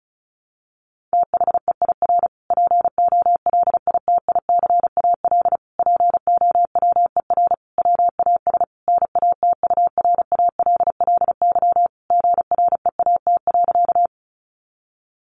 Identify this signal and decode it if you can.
Morse code, heard as a keyed tone, about 34 wpm, 700 Hz. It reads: THEIR POLITICAL POWER WAS NATURALLY GREAT.
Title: Sentences from Top 1300 Words 35wpm